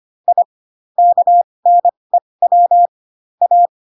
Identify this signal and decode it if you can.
Morse code, heard as a keyed tone, about 25 wpm, spaced as for 20 wpm, 700 Hz. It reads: I KNEW A
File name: Pinocchio / Pinocchio ch3 15 25 ik1hge